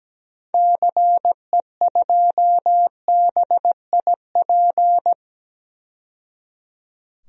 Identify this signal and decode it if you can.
Morse code, heard as a keyed tone, about 17 wpm, 700 Hz. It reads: CE2BIP
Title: DXCC-100 Call Signs - 6 Characters - 17wpm